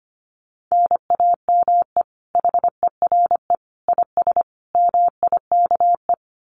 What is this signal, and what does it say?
Morse code, heard as a keyed tone, about 25 wpm, 700 Hz. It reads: NAME HERE IS MIKE
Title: Common QSO Phrases 25wpm